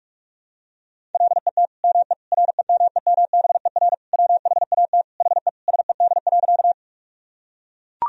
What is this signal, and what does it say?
Morse code, heard as a keyed tone, about 41 wpm, 700 Hz. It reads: LET ME REMEMBER WHAT HE SED.
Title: Sentences from Top 400 Words and Abbreviations 45wpm